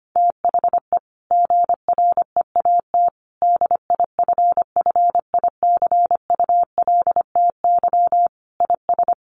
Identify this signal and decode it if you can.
Morse code, heard as a keyed tone, about 25 wpm, 700 Hz. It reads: THE GREAT DIFFICULTY IS
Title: Sentences from Top 1500 Words 25wpm